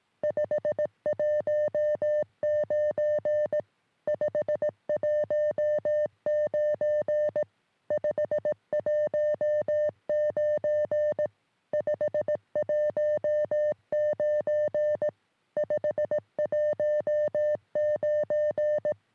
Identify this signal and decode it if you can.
Morse code, heard as a keyed tone, about 18 wpm, 600 Hz. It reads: 519 519 519 519 519
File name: sm7lcb / ok2poi 3cm